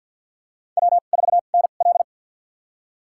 Morse code, heard as a keyed tone, about 53 wpm, spaced as for 37 wpm, 700 Hz. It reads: W4NL
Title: USA Call Signs - 4 Characters - 2x Character Spacing - 50wpm